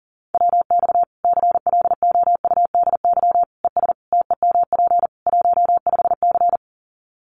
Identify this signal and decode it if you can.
Morse code, heard as a keyed tone, about 40 wpm, 700 Hz. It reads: WX CLOUDY ES TEMP 15C
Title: Common QSO Phrases 40wpm